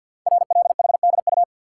Morse code, heard as a keyed tone, about 48 wpm, 690 Hz. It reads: RLHDU
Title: Code Groups - 5 Letters - 50wpm